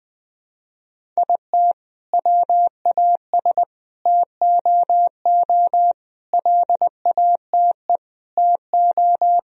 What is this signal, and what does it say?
Morse code, heard as a keyed tone, about 21 wpm, 700 Hz. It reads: IT WAS TOO LATE TO
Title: Sentences from Top 1000 Words - Review Part 1 - 20wpm